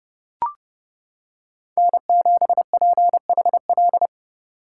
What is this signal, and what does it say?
Morse code, heard as a keyed tone, about 28 wpm, 700 Hz. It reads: N7PHL